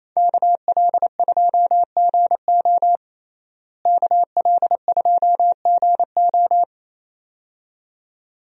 Morse code, heard as a keyed tone, about 28 wpm, 700 Hz. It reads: KL2GO KL2GO
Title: USA Call Signs - 5 Characters - Repeated 2x - 28wpm